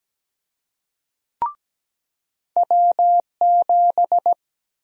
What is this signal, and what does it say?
Morse code, heard as a keyed tone, about 17 wpm, 700 Hz. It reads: W7
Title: USA Call Sign Prefixes - 17wpm